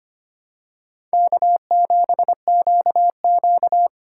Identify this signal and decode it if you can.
Morse code, heard as a keyed tone, about 25 wpm, 700 Hz. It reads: K7QQ
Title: USA Call Signs - 4 Characters - 3x Character Spacing - 25wpm